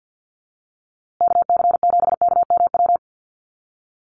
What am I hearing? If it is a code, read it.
Morse code, heard as a keyed tone, about 50 wpm, 700 Hz. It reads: KC7KMW
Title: USA Call Signs Repeated 3x 50wpm